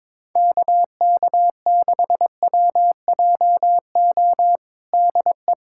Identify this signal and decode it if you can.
Morse code, heard as a keyed tone, about 22 wpm, 700 Hz. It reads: KK6WJO DE
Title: USA Callsign DE Callsign 22wpm